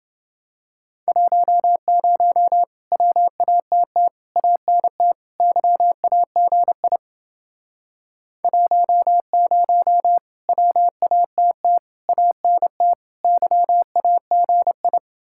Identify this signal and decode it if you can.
Morse code, heard as a keyed tone, about 30 wpm, 700 Hz. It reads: 10 WATT ANT YAGI 10 WATT ANT YAGI